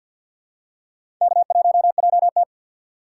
Morse code, heard as a keyed tone, about 49 wpm, 700 Hz. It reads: K1JT
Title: USA Call Signs - 3x Character Spacing - 50wpm